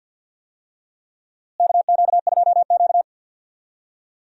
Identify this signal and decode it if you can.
Morse code, heard as a keyed tone, about 51 wpm, 700 Hz. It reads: KQ2Q
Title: USA Call Signs - 4 Characters - Repeated 2x - 50wpm